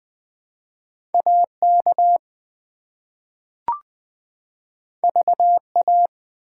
Morse code, heard as a keyed tone, about 20 wpm, 700 Hz. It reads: AK VA
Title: ARRL Sections v2 - 20wpm